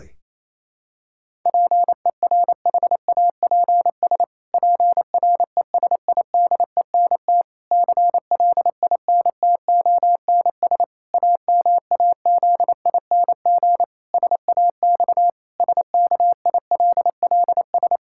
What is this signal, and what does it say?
Morse code, heard as a keyed tone, about 28 wpm, 700 Hz. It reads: PERHAPS PRESIDENT CLINTONS AMAZING SAX SKILLS
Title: Pangram Challenge v2 - 28wpm